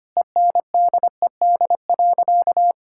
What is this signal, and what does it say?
Morse code, heard as a keyed tone, about 25 wpm, 700 Hz. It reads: ENDED.
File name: practice / 25wpm / Sentences with 2-Word Phrases - 25wpm